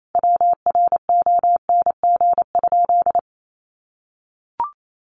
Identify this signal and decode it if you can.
Morse code, heard as a keyed tone, about 28 wpm, 700 Hz. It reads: WRONG?
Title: A Sound Of Thunder 28wpm